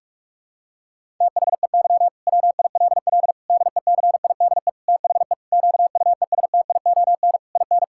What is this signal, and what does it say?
Morse code, heard as a keyed tone, about 41 wpm, 700 Hz. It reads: THEY WILL DECIDE THE QUESTION IN
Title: Sentences from Top 1600 Words 45wpm